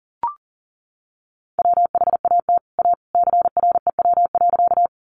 Morse code, heard as a keyed tone, about 40 wpm, 700 Hz. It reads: WHAT A CREW.
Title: Sentences from Top 1500 Words 40wpm